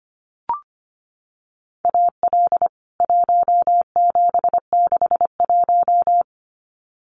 Morse code, heard as a keyed tone, about 25 wpm, 700 Hz. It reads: AL 1761
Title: CWT Exchange 25wpm